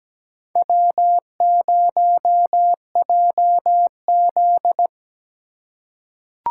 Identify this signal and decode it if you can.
Morse code, heard as a keyed tone, about 17 wpm, 700 Hz. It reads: W0JZ
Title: USA Call Signs - 4 Characters - 2x Character Spacing - 17wpm